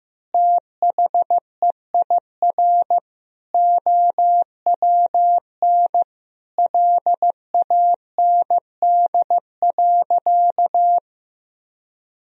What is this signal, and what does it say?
Morse code, heard as a keyed tone, about 15 wpm, 700 Hz. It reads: THEIR OWN LAND.
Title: Sentences from Top 1000 Words - Review Part 2 - 15wpm